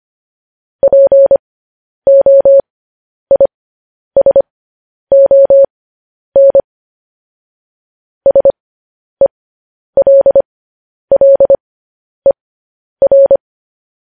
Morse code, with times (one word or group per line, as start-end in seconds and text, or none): 0.8-6.6 POISON
8.3-13.4 SELLER